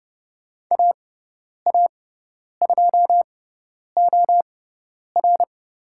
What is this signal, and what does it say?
Morse code, heard as a keyed tone, about 30 wpm, spaced as for 10 wpm, 700 Hz. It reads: AA2OR
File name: Lesson 16g - Letter c - Call Signs 30wpm